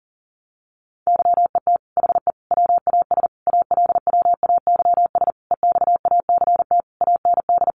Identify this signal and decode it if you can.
Morse code, heard as a keyed tone, about 40 wpm, 700 Hz. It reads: YET HE WAS ALWAYS EXACT AND
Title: Sentences from Top 1300 Words 40wpm